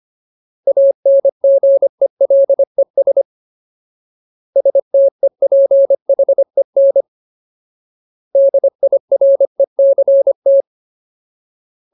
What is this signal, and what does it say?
Morse code, heard as a keyed tone, about 25 wpm, 550 Hz. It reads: ANGELES STEPHEN DIRECT
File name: Ultimate Word List / 25WPM Ultimate